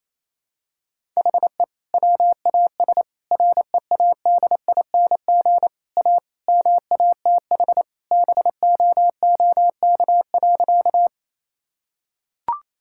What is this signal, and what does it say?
Morse code, heard as a keyed tone, about 28 wpm, 700 Hz. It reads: HE WAS READING A MATH BOOK.
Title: Sentences from Top 1000 Words - Review Part 6 - 28wpm